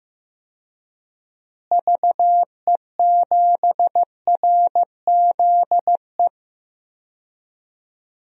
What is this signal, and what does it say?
Morse code, heard as a keyed tone, about 15 wpm, 700 Hz. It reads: VE7RZE
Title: DXCC-100 Call Signs - 6 Characters - Repeated 3x - 15wpm